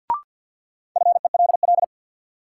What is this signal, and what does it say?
Morse code, heard as a keyed tone, about 51 wpm, 700 Hz. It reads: FELL
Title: Sets of 5 Words Encore - New Word Drill - 50wpm